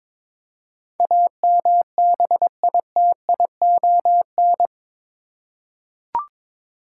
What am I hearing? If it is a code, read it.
Morse code, heard as a keyed tone, about 22 wpm, 700 Hz. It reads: AMBITION